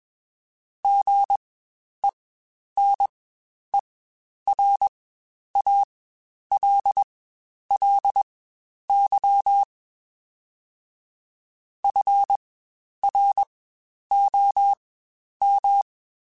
Morse code, heard as a keyed tone, about 21 wpm, spaced as for 10 wpm, 800 Hz. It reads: GENERALLY FROM
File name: Spaced Characters / Spaced Characters-mp3s / Weather5 22 sp @20